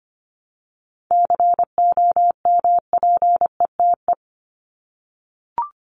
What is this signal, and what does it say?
Morse code, heard as a keyed tone, about 25 wpm, 700 Hz. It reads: COMPETE
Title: Top 1751-2000 Words 25wpm